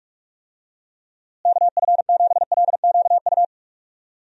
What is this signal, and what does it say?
Morse code, heard as a keyed tone, about 44 wpm, 700 Hz. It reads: KF7LQU
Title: USA Call Signs - 6 Characters - Repeated 2x - 45wpm